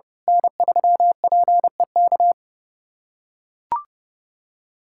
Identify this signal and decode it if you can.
Morse code, heard as a keyed tone, about 30 wpm, 700 Hz. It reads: N3PEK
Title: USA Callsign DE Callsign 30wpm